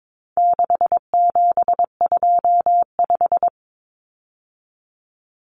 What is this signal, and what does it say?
Morse code, heard as a keyed tone, about 22 wpm, 700 Hz. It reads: 6725
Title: Numbers - 4-Digit 22wpm